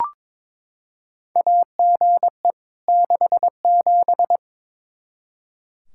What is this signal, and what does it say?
Morse code, heard as a keyed tone, about 22 wpm, 700 Hz. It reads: AGE 67
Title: Practice QSO - Exchange 3 - More Info - 22wpm